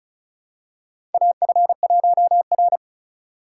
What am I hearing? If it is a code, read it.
Morse code, heard as a keyed tone, about 36 wpm, 700 Hz. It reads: AF1R
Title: USA Call Signs - 4x Character Spacing - 35wpm